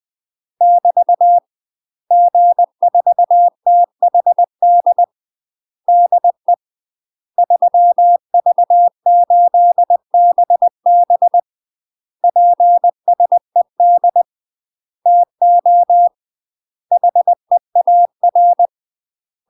Morse code, heard as a keyed tone, about 20 wpm, 700 Hz. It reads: <BT> G4THD DE 3V8BB PSED TO HEAR
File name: UK MASTER QSOs TRAINING / 20WPM / UK MASTER QSOs 0034